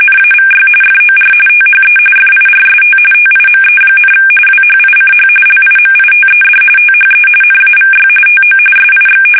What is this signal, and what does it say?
RTTY, about 45.45 baud. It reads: NAVY FOR THE FIRST TIME, WHICH LIMITED THE NAVY TO SIX
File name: fsk850 / usnavy1